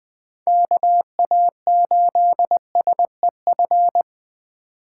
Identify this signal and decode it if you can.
Morse code, heard as a keyed tone, about 21 wpm, 700 Hz. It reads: KA8SEF
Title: USA Call Signs - 6 Characters - 4x Character Spacing - 20wpm